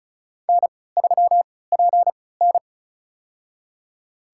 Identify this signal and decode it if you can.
Morse code, heard as a keyed tone, about 36 wpm, spaced as for 20 wpm, 700 Hz. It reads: N3PN
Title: USA Call Signs - 4 Characters - 3x Character Spacing - 35wpm